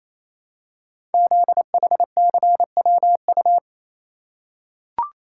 Morse code, heard as a keyed tone, about 28 wpm, 700 Hz. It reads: ZHCWU
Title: Code Groups - 5 Letters - 28wpm